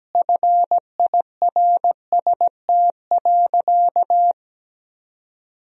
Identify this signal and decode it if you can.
Morse code, heard as a keyed tone, about 17 wpm, 700 Hz. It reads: FIRST.
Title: Sentences from Top 1000 Words - Review Part 1 - 17wpm